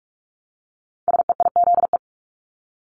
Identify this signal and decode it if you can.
Morse code, heard as a keyed tone, about 46 wpm, 700 Hz. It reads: SEIZE
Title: Multi-letter Phonemes Repeated 3x 45wpm